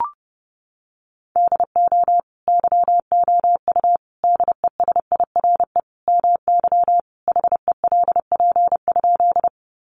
Morse code, heard as a keyed tone, about 30 wpm, 700 Hz. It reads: DO YOU DESIRE MY HELP?
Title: Sentences from Top 1100 Words 30wpm